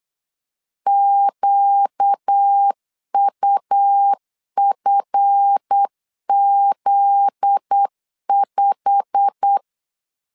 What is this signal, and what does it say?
Morse code, heard as a keyed tone, about 8 wpm, 800 Hz. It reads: QUFZ5
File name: LEZIONI / 10 SIGNS / 10D COMA 8wpm 3749